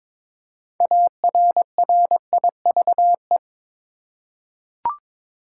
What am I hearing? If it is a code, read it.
Morse code, heard as a keyed tone, about 22 wpm, 700 Hz. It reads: ARRIVE
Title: Six Letter Words - 22wpm